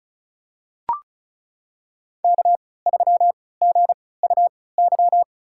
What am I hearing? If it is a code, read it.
Morse code, heard as a keyed tone, about 36 wpm, spaced as for 20 wpm, 700 Hz. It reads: K3GUY
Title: USA Call Signs - 3x Character Spacing - 35wpm